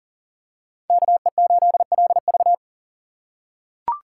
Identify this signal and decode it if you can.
Morse code, heard as a keyed tone, about 40 wpm, 700 Hz. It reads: KE8LV
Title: USA Call Signs - 5 Characters - 6x Character Spacing - 40wpm